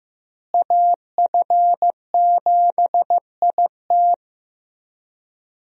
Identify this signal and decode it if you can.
Morse code, heard as a keyed tone, about 15 wpm, 700 Hz. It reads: AF7IT
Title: USA Call Signs - 5 Characters - Repeated 3x - 15wpm